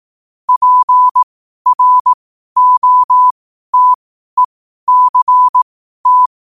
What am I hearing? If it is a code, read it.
Morse code, heard as a keyed tone, about 18 wpm, spaced as for 12 wpm, 1000 Hz. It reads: PROTECT
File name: qotdfiles / 12wpm